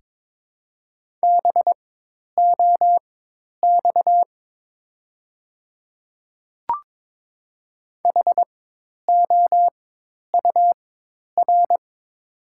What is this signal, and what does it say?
Morse code, heard as a keyed tone, about 23 wpm, spaced as for 10 wpm, 700 Hz. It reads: BOX HOUR
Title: 4x Character Spacing - Top 500 Words - 22wpm